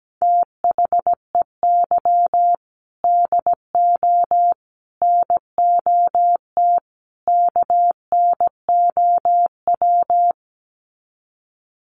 Morse code, heard as a keyed tone, about 17 wpm, 700 Hz. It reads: THEY DO NOT KNOW
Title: Sets of 4 Words 17wpm